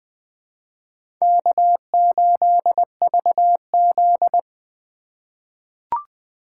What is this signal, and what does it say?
Morse code, heard as a keyed tone, about 20 wpm, 700 Hz. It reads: K8VZ
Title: USA Call Signs - 2x Character Spacing - 20wpm